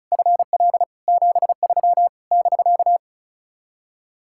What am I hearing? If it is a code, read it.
Morse code, heard as a keyed tone, about 43 wpm, 700 Hz. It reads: FL 73 <BK>
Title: Speed-Racing Common QSO Phrases - 35wpm